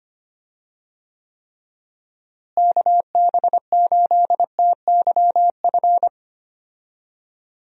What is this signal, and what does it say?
Morse code, heard as a keyed tone, about 25 wpm, 700 Hz. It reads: KB8TYF
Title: USA Call Signs - 6 Characters - Repeated 2x - 25wpm